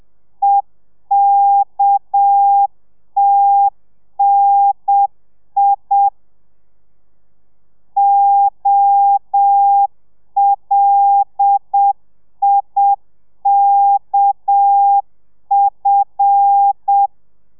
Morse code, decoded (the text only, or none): EKTNI OLIKF